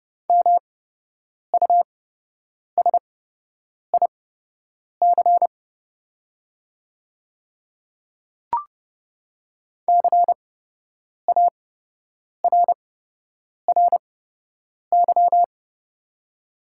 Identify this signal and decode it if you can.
Morse code, heard as a keyed tone, about 30 wpm, spaced as for 8 wpm, 700 Hz. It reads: MUSIC CARRY
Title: 8x Character Spacing - Top 500 Words - 30wpm